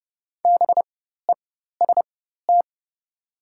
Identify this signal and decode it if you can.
Morse code, heard as a keyed tone, about 29 wpm, spaced as for 14 wpm, 690 Hz. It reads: BEST
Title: 4x Character Spacing - Top 500 Words - 30wpm